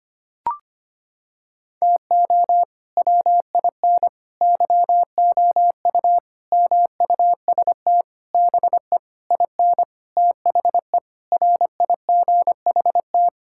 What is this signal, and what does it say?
Morse code, heard as a keyed tone, about 25 wpm, 700 Hz. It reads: TO WIN YOU MUST BE IN THE RIGHT
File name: practice / 25wpm / Sentences with 9-Word Phrases - 25wpm